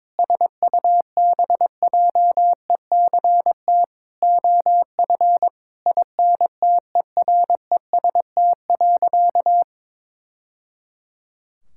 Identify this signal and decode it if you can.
Morse code, heard as a keyed tone, about 22 wpm, 700 Hz. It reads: SUBJECT OF INTEREST.
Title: Sentences from Top 1000 Words - Review Part 5 - 22wpm